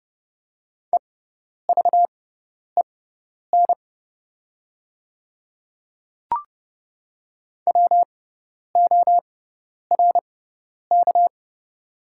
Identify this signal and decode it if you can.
Morse code, heard as a keyed tone, about 30 wpm, spaced as for 10 wpm, 700 Hz. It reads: EVEN WORK
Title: 6x Character Spacing - Top 300 Words - 30wpm